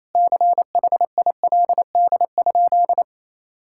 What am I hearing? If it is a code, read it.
Morse code, heard as a keyed tone, about 28 wpm, 700 Hz. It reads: CHILD?
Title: Sentences from Top 1600 Words 28wpm